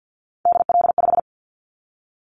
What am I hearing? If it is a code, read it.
Morse code, heard as a keyed tone, about 50 wpm, 700 Hz. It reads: DL5
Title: Callsign Prefixes for Top 25 DXCC - 50wpm